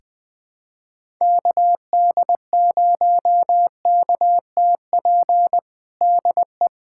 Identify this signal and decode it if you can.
Morse code, heard as a keyed tone, about 20 wpm, 700 Hz. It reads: KD0KTP DE
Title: USA Callsign DE Callsign 20wpm